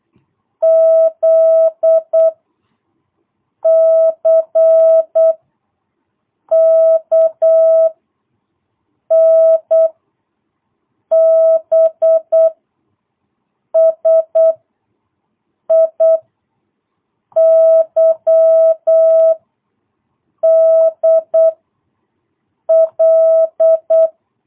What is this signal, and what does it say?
Morse code, heard as a keyed tone, about 8 wpm, 650 Hz. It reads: Z C K N B S I Y D L
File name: jh8cbh / 2022cw